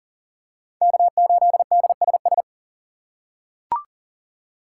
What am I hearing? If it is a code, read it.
Morse code, heard as a keyed tone, about 39 wpm, 700 Hz. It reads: K8DSS
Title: K1USN SST QSO Practice - 40wpm